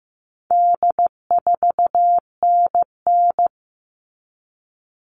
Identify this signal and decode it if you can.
Morse code, heard as a keyed tone, about 15 wpm, 700 Hz. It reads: D4NN
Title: All Cut Numbers - 4-Digit 15wpm